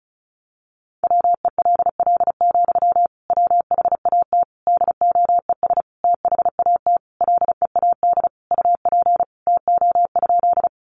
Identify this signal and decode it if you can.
Morse code, heard as a keyed tone, about 34 wpm, 700 Hz. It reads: WELL, WHAT DOES THAT LEAD UP TO?
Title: Sentences from Top 500 Words 35wpm